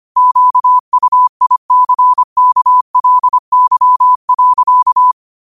Morse code, heard as a keyed tone, about 25 wpm, 1000 Hz. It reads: QUICKLY.
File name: qotdfiles / 25wpm